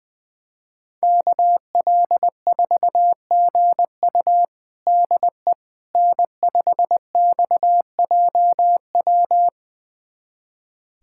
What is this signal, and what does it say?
Morse code, heard as a keyed tone, about 20 wpm, 700 Hz. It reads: KL4GU DE N5XJW
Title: USA Callsign DE Callsign - 5 Characters - 20wpm